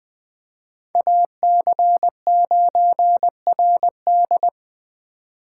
Morse code, heard as a keyed tone, about 21 wpm, 700 Hz. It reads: AC9RD
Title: Speed Racing - USA Call Signs 17wpm